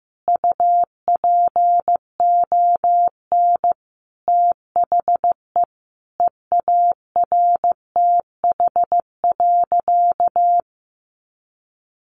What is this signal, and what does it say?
Morse code, heard as a keyed tone, about 15 wpm, 700 Hz. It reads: UPON THE EARTH.
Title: Sentences from Top 1200 Words 15wpm